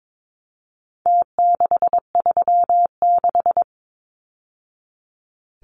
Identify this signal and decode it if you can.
Morse code, heard as a keyed tone, about 22 wpm, 700 Hz. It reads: T636
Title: Numbers - Contest Style - 4-Digit 22wpm